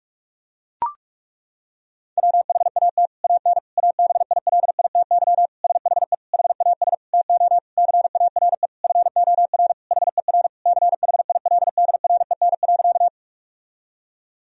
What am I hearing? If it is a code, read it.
Morse code, heard as a keyed tone, about 41 wpm, 700 Hz. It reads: WHAT AN ABILITY SHE HAS TO CARE FOR HER CHILDREN.
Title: Sentences from Top 1000 Words - Review Part 3 - 45wpm